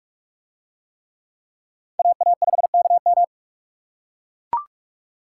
Morse code, heard as a keyed tone, about 44 wpm, 700 Hz. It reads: AA5KM